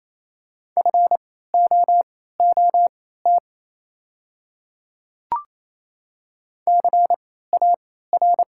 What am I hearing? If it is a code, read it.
Morse code, heard as a keyed tone, about 28 wpm, spaced as for 16 wpm, 690 Hz. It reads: FOOT CAR
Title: 3x Character Spacing - Top 500 Words - 28wpm